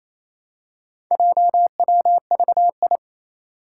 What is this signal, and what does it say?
Morse code, heard as a keyed tone, about 28 wpm, 700 Hz. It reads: JWVI
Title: Four Challenging Letter-Number Combinations 28wpm